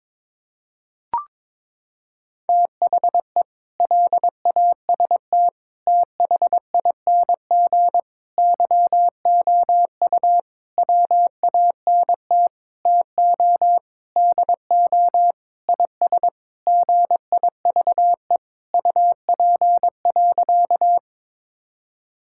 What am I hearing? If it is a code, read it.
Morse code, heard as a keyed tone, about 21 wpm, 700 Hz. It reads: THE LAST THING YOU WANT TO DO IS GIVE UP.
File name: practice / 22wpm / Sentences with 8-Word Phrases - 22wpm